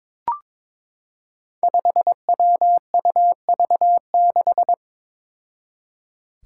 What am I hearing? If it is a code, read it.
Morse code, heard as a keyed tone, about 22 wpm, 700 Hz. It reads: 5WUV6
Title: Five Challenging Letter-Number Combinations 22wpm